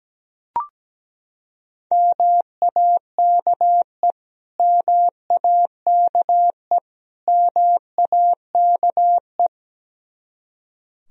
Morse code, heard as a keyed tone, about 17 wpm, 700 Hz. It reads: MAKE MAKE MAKE
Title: Top 100 Words Repeated 17wpm